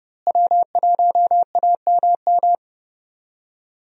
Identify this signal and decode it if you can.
Morse code, heard as a keyed tone, about 30 wpm, 700 Hz. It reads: W1AMM
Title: USA Call Signs - 5 Characters - Repeated 3x - 30wpm